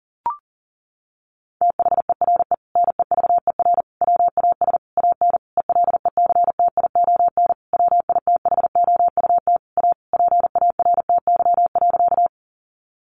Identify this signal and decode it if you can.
Morse code, heard as a keyed tone, about 40 wpm, 700 Hz. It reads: THERE NEVER WAS AN ELECTION WITHOUT A PARTY.
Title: Sentences from Top 1500 Words 40wpm